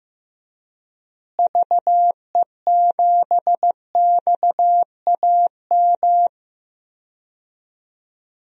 Morse code, heard as a keyed tone, about 15 wpm, 700 Hz. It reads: VE7XAM
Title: DXCC-100 Call Signs - 6 Characters - Repeated 3x - 15wpm